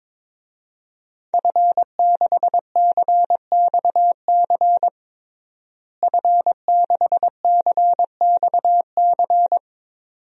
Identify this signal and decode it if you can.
Morse code, heard as a keyed tone, about 21 wpm, 700 Hz. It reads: F6CXC F6CXC
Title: DXCC-100 Call Signs - 5 Characters - Repeated 2x - 22wpm